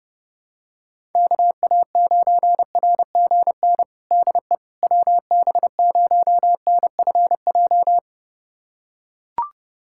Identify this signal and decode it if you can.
Morse code, heard as a keyed tone, about 30 wpm, 700 Hz. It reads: KA9RGN DE WB0NFJ
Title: USA Callsign DE Callsign - 6 Characters - 30wpm